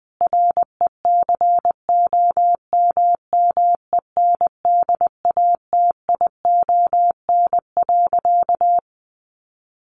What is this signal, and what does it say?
Morse code, heard as a keyed tone, about 20 wpm, 700 Hz. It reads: RECOMMENDATION.
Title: Sentences from Top 1400 Words 20wpm